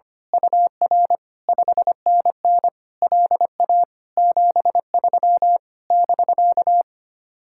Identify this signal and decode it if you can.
Morse code, heard as a keyed tone, about 24 wpm, 700 Hz. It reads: UR 5NN LA 73 <BK>
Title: Speed-Racing Common QSO Phrases Encore - 25wpm